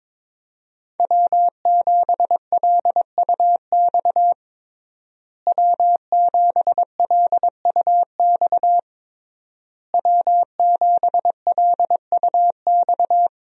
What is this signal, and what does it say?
Morse code, heard as a keyed tone, about 22 wpm, 700 Hz. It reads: W7LUX W7LUX W7LUX
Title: USA Call Signs - 5 Characters - Repeated 3x - 22wpm